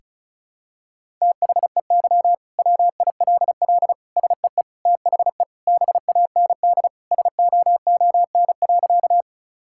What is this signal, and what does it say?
Morse code, heard as a keyed tone, about 36 wpm, 700 Hz. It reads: THEY WILL SEE THE BAND SOON.
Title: Sentences from Top 1600 Words 35wpm